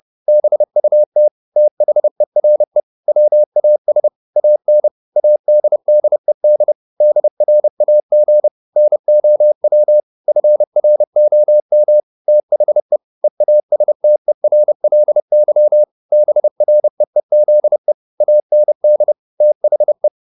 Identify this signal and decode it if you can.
Morse code, heard as a keyed tone, about 30 wpm, 600 Hz. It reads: BUT THERE WAS AN ADDED DRAG NOW FROM THE EASTERLY BREEZE AND THE
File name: the old man and the sea / the old man and the sea 1 30wpm